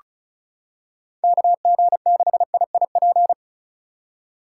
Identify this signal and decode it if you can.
Morse code, heard as a keyed tone, about 34 wpm, 700 Hz. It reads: KG6IIP
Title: K1USN SST QSO Practice - 35wpm